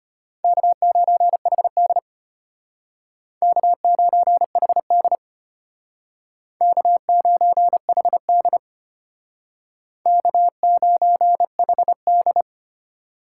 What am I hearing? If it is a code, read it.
Morse code, heard as a keyed tone, about 32 wpm, 700 Hz. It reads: K9HD K9HD K9HD K9HD